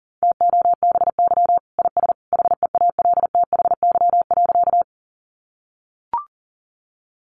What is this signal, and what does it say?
Morse code, heard as a keyed tone, about 40 wpm, 700 Hz. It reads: TOBY IS HEALTHY.
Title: Easy Phrases 40wpm